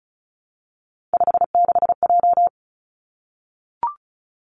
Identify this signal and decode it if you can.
Morse code, heard as a keyed tone, about 36 wpm, 700 Hz. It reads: 56J
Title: Three Challenging Letter-Number Combinations 35wpm